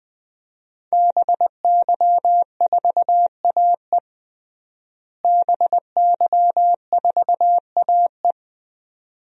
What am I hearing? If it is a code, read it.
Morse code, heard as a keyed tone, about 20 wpm, 700 Hz. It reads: BY4AE BY4AE
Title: DXCC-100 Call Signs - 5 Characters - Repeated 2x - 20wpm